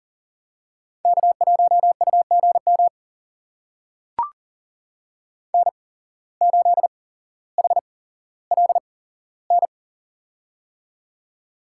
Morse code, heard as a keyed tone, about 39 wpm, spaced as for 17 wpm, 700 Hz. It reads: K1UGM N8HLN
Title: USA Call Signs - 5 Characters - 8x Character Spacing - 40wpm